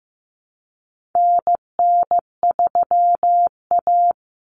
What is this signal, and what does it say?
Morse code, heard as a keyed tone, about 15 wpm, 700 Hz. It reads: NN3A
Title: Common Cut Numbers - 4-Digit 15wpm